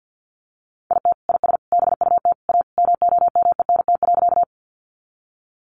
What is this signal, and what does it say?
Morse code, heard as a keyed tone, about 50 wpm, 700 Hz. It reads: IT IS BUT A MOMENT.
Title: Sentences from Top 1000 Words 50wpm